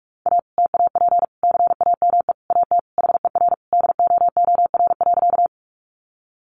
Morse code, heard as a keyed tone, about 44 wpm, 700 Hz. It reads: A TAP CAME AT HER DOOR.
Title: Sentences from Top 1400 Words 45wpm